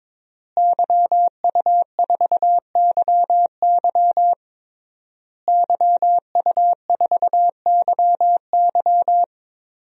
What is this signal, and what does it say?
Morse code, heard as a keyed tone, about 22 wpm, 700 Hz. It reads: YU4YY YU4YY
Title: DXCC-100 Call Signs - 5 Characters - Repeated 3x - 22wpm